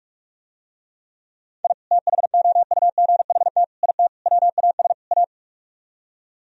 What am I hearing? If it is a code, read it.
Morse code, heard as a keyed tone, about 47 wpm, 700 Hz. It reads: I THOUGHT IT WAS A